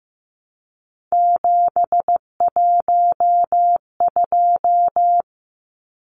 Morse code, heard as a keyed tone, about 15 wpm, 700 Hz. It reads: 712
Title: Numbers - 3-Digit 15wpm